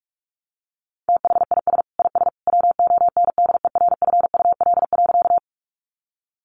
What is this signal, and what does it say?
Morse code, heard as a keyed tone, about 44 wpm, 700 Hz. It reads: THIS IS WONDERFUL.
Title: Sentences from Top 1100 Words 45wpm